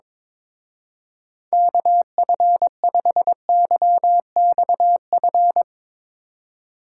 Morse code, heard as a keyed tone, about 22 wpm, 700 Hz. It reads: KF5YXF
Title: Speed Racing - USA 2x3 Call Signs 22wpm